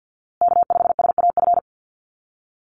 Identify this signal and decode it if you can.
Morse code, heard as a keyed tone, about 50 wpm, 700 Hz. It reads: K5SAF
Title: USA Call Signs Repeated 2x 50wpm